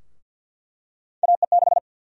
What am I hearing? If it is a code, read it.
Morse code, heard as a keyed tone, about 53 wpm, 700 Hz. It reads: AE6
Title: USA Call Sign Prefixes - 50wpm